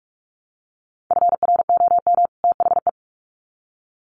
Morse code, heard as a keyed tone, about 44 wpm, 700 Hz. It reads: FROM THE
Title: Sets of 2 Words 45wpm